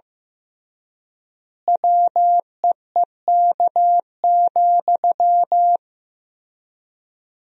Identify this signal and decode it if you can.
Morse code, heard as a keyed tone, about 15 wpm, 700 Hz. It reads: WEEK,
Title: 3x Word-Spacing - Sentences from Top 500 Words - 15wpm